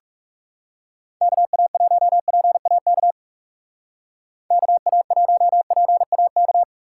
Morse code, heard as a keyed tone, about 36 wpm, 700 Hz. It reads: KA1PAK KA1PAK
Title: Speed Racing - USA 2x3 Call Signs 30wpm